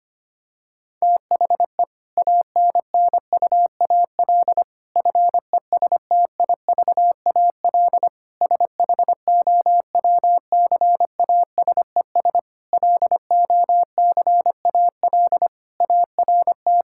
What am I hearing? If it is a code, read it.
Morse code, heard as a keyed tone, about 25 wpm, 700 Hz. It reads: THE ANNUAL FESTIVAL SHOWCASES LOCAL ART
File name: practice / 25wpm / Difficult Sentences with Hints - 25wpm